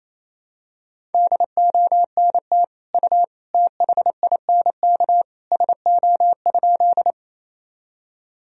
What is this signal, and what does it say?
Morse code, heard as a keyed tone, about 28 wpm, 700 Hz. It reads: DONT U THINK SO?
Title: Sentences from Top 300 Words and Abbreviations 28wpm